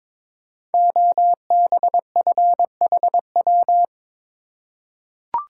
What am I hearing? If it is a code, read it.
Morse code, heard as a keyed tone, about 22 wpm, 690 Hz. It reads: OBFHW
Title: Code Groups - 5 Letters - 22wpm